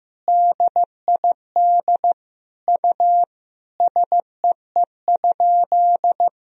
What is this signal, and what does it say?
Morse code, heard as a keyed tone, about 15 wpm, 700 Hz. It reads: DID U SEE?
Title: Sentences from Top 100 Words and Abbreviations 15wpm